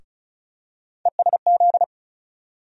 Morse code, heard as a keyed tone, about 36 wpm, 700 Hz. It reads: ESZ
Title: USA Call Sign Postfixes - 3 Letters 35wpm